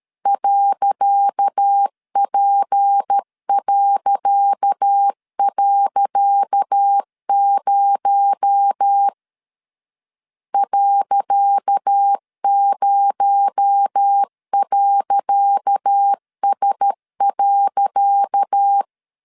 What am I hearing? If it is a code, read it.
Morse code, heard as a keyed tone, about 13 wpm, 800 Hz. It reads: .P..0 .0.S.
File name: LEZIONI / 10 SIGNS / 10C DOT 12wpm 2451